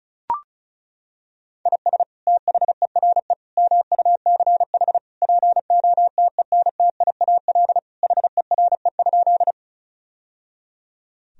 Morse code, heard as a keyed tone, about 35 wpm, 700 Hz. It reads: IS THERE MUCH POTENTIAL HERE?
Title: Sentences from Top 1000 Words - Review Part 4 - 35wpm